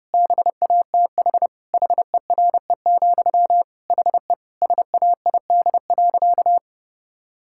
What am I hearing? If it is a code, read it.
Morse code, heard as a keyed tone, about 30 wpm, 700 Hz. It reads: BATH HERE, HE SAID.
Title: Sentences from Top 1600 Words 30wpm